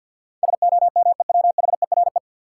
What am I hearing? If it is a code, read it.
Morse code, heard as a keyed tone, about 49 wpm, 700 Hz. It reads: SOMEWHERE
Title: Sentences from Top 1400 Words - New Word Drill Alt - 50wpm